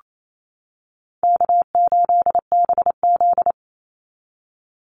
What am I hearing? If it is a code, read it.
Morse code, heard as a keyed tone, about 28 wpm, 700 Hz. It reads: K8BZ
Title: Easy Call Signs - 28wpm